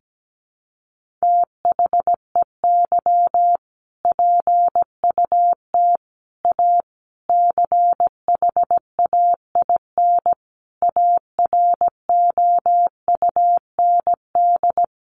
Morse code, heard as a keyed tone, about 17 wpm, 700 Hz. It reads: THEY PUT A CHAIN AROUND
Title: Sentences from Top 1500 Words 17wpm